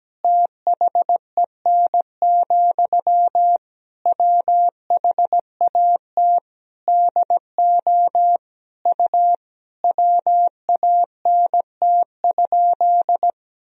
Morse code, heard as a keyed tone, about 17 wpm, 700 Hz. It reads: THEN, WHAT DO U WANT?
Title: Sentences from Top 200 Words and Abbreviations 17wpm